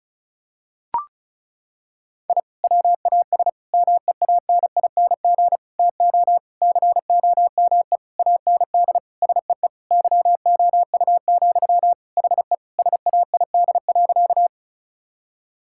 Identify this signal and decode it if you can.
Morse code, heard as a keyed tone, about 36 wpm, 700 Hz. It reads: I WAS MEANING TO COME AND SEE YOU, HE SAID.
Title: Sentences from Top 1000 Words - Review Part 6 - 35wpm